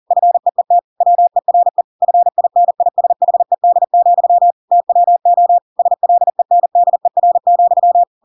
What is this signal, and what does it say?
Morse code, heard as a keyed tone, about 40 wpm, 700 Hz. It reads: FEET WERE FINISHED, TWO SLENDER,
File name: Pinocchio / Pinocchio ch8 40 40 ik1hge